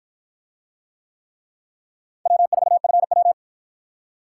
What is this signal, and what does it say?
Morse code, heard as a keyed tone, about 48 wpm, 700 Hz. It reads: W4FW